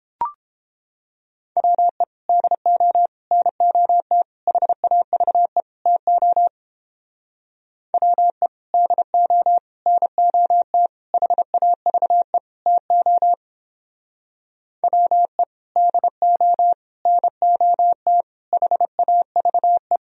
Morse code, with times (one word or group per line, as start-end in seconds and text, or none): 1.6-2.0 WE
2.3-3.1 DO
3.3-4.2 NOT
4.5-5.6 HAVE
5.9-6.5 TO
7.9-8.5 WE
8.7-9.6 DO
9.9-10.9 NOT
11.1-12.4 HAVE
12.7-13.3 TO
14.8-15.4 WE
15.8-16.7 DO
17.1-18.2 NOT
18.5-20.0 HAVE